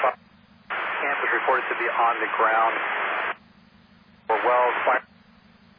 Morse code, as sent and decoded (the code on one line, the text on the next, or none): none